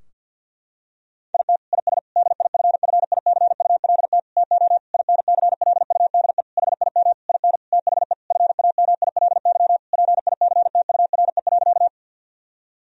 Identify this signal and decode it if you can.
Morse code, heard as a keyed tone, about 50 wpm, 700 Hz. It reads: IT IS DIFFICULT TO INCLUDE HIM IN THE FAMILY PICTURE.
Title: Sentences from Top 1000 Words - Review Part 3 - 50wpm